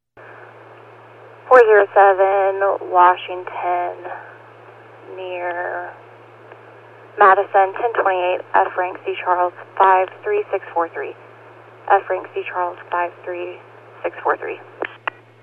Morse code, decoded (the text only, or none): none